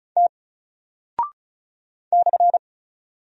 Morse code, heard as a keyed tone, about 36 wpm, 700 Hz. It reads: T /
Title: Single Character (Letter:Number:Punctuation:Prosign) Alt 35wpm